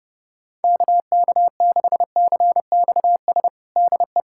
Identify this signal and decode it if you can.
Morse code, heard as a keyed tone, about 30 wpm, 700 Hz. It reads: KK6CXS DE
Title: USA Callsign DE Callsign - 6 Characters - 30wpm